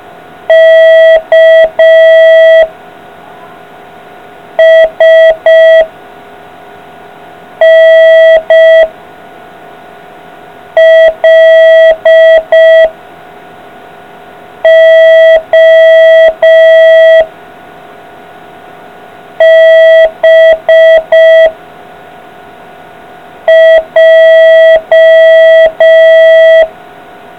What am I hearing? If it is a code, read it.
Morse code, heard as a keyed tone, about 5 wpm, 650 Hz. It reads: K S N L O B J